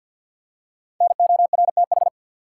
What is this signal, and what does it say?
Morse code, heard as a keyed tone, about 50 wpm, 700 Hz. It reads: NORTH
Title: Five Letter Words - 50wpm